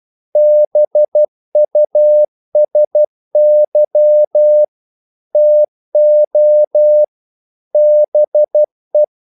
Morse code, heard as a keyed tone, about 12 wpm, 600 Hz. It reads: BUSY TO BE
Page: English quotations in CW (telegraphy - morse)